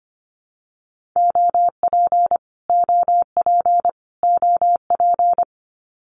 Morse code, heard as a keyed tone, about 25 wpm, 700 Hz. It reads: OP OP OP
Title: Abbreviations Repeated 3x - 25wpm